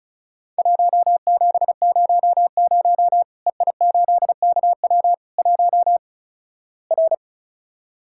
Morse code, heard as a keyed tone, about 36 wpm, 700 Hz. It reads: 1700 EI8KW 1 R
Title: Worked All Europe DX Contest - QTC Exchange - 35wpm